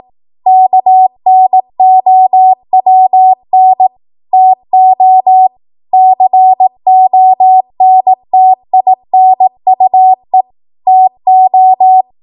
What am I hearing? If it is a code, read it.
Morse code, heard as a keyed tone, about 18 wpm, 750 Hz. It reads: KNOWN TO CONTINUE TO
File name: Morse / 18WPM